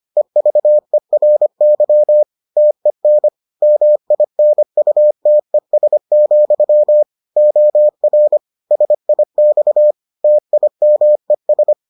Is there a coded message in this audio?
Yes: Morse code from 0.2 to 11.7 s, heard as a keyed tone, about 25 wpm, 600 Hz. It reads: EVERY TEN MINUTES, OR SIX TIMES